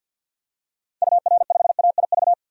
Morse code, heard as a keyed tone, about 50 wpm, 700 Hz. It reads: UR5AIV
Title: DXCC-100 Call Signs - 6 Characters - 50wpm